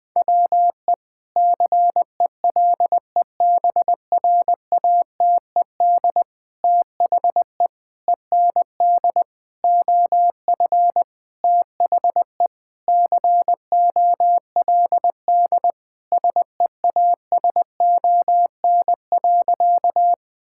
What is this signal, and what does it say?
Morse code, heard as a keyed tone, about 20 wpm, 700 Hz. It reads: WE CELEBRATED THE END OF THE COLD SEASON.
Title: Sentences with 5-Word Phrases - 20wpm